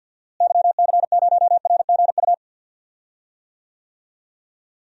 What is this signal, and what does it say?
Morse code, heard as a keyed tone, about 53 wpm, 700 Hz. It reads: YC0RGU
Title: DXCC-100 Call Signs - 6 Characters - Repeated 2x - 50wpm